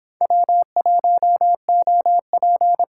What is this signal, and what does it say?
Morse code, heard as a keyed tone, about 29 wpm, 700 Hz. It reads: W1OP